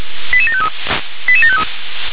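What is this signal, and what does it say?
Five-tone selective call (ZVEI), heard as quick runs of five tones, tones of about 70 ms. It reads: 88753 88753
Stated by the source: Five-tone ZVEI selective call, received via SDR.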